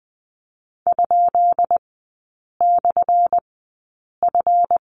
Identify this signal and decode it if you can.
Morse code, heard as a keyed tone, about 20 wpm, spaced as for 8 wpm, 700 Hz. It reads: ?/F
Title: 191 Lesson 31e - Letter v - Three Characters 20wpm